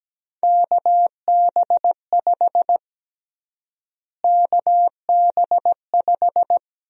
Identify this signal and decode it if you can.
Morse code, heard as a keyed tone, about 17 wpm, 700 Hz. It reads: KB5 KB5
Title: USA Call Sign Prefixes - Repeated 2x - 17wpm